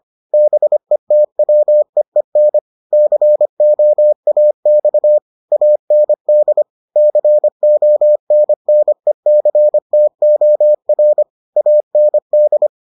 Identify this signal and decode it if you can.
Morse code, heard as a keyed tone, about 25 wpm, 600 Hz. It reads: BETWEEN COAX AND CONNECTOR AND